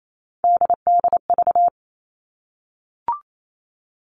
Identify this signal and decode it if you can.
Morse code, heard as a keyed tone, about 28 wpm, 700 Hz. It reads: DDV
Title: All Cut Numbers - 3-Digit 28wpm